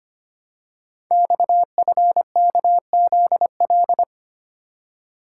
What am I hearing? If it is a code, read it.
Morse code, heard as a keyed tone, about 25 wpm, 700 Hz. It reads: XFKZL
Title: Code Groups - 5 Letters - 25wpm